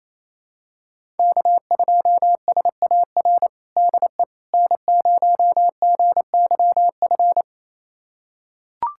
Morse code, heard as a keyed tone, about 28 wpm, 700 Hz. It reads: K2SAR DE N0GYF
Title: USA Callsign DE Callsign - 5 Characters - 28wpm